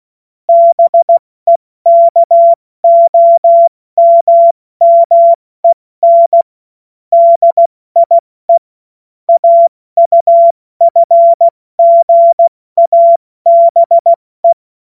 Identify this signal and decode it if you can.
Morse code, heard as a keyed tone, about 16 wpm, 680 Hz. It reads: BEKOMMEN DIE AUFGABE
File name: Morsekurs / Witze / Witz 21. Dezember 2021, 70 F80